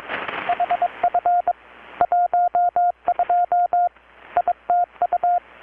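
Morse code, heard as a keyed tone, about 23 wpm, 720 Hz, with radio noise. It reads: HF 12 ITU
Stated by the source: Morse from KPH's Night of Nights XX maritime radiotelegraphy event, received in Chiapas, Mexico about 3,800 km away on a Kenwood R-600 with a grounded 10 m long-wire antenna.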